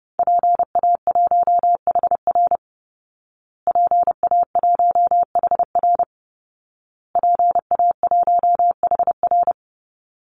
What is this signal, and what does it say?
Morse code, heard as a keyed tone, about 28 wpm, 700 Hz. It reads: PA1HR PA1HR PA1HR
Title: Call Signs Repeated 2x - 30wpm